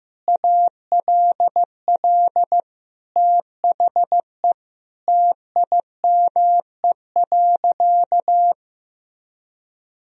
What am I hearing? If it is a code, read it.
Morse code, heard as a keyed tone, about 15 wpm, 700 Hz. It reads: ALL THE TIME.
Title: Sentences from Top 1000 Words - Review Part 3 - 15wpm